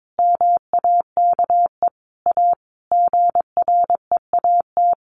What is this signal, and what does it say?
Morse code, heard as a keyed tone, about 22 wpm, 700 Hz. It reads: MAKE A GREAT
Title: Sentences from Top 800 Words 22wpm